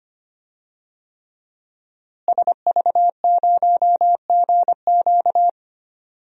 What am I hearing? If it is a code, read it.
Morse code, heard as a keyed tone, about 25 wpm, 700 Hz. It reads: SV0GQ
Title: DXCC-100 Call Signs - 5 Characters - Repeated 3x - 25wpm